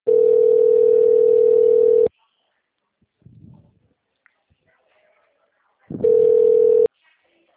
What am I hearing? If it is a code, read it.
Ringback tone (North America), 440 + 480 Hz.
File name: out- -2024- - - .6169